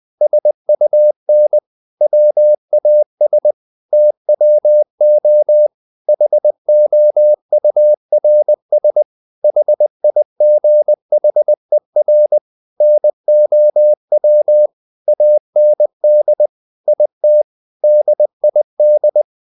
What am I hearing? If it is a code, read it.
Morse code, heard as a keyed tone, about 20 wpm, 600 Hz. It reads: SUN WAS TWO HOURS HIGHER NOW AND IT DID
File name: the old man and the sea / the old man and the sea part 1 20wpm